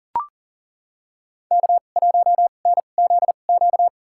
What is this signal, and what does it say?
Morse code, heard as a keyed tone, about 40 wpm, spaced as for 29 wpm, 700 Hz. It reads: K1NZQ
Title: USA Call Signs - 5 Characters - 2x Character Spacing - 40wpm